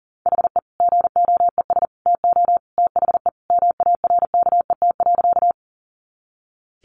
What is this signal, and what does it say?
Morse code, heard as a keyed tone, about 40 wpm, 700 Hz. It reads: HE GOES TO THE MARKET.
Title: Additional Easy Phrases 40wpm